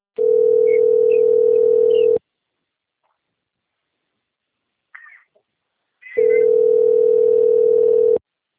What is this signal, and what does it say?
Ringback tone (North America), 440 + 480 Hz.
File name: rg-1600-unknown- - - .40355